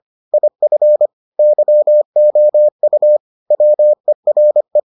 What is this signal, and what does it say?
Morse code, heard as a keyed tone, about 25 wpm, 600 Hz. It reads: IF YOU WERE